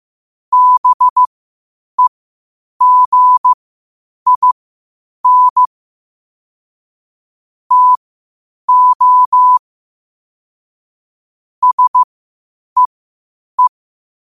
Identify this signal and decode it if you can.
Morse code, heard as a keyed tone, about 15 wpm, spaced as for 8 wpm, 1000 Hz. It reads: BEGIN TO SEE
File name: qotdfiles / 8wpm